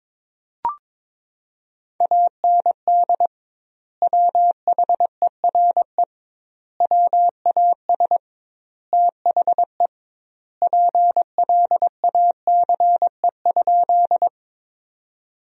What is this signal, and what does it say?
Morse code, heard as a keyed tone, about 22 wpm, 700 Hz. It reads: AND WHERE WAS THE PLACE?
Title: 2x Word-Spacing - Sentences from Top 200 Words - 22wpm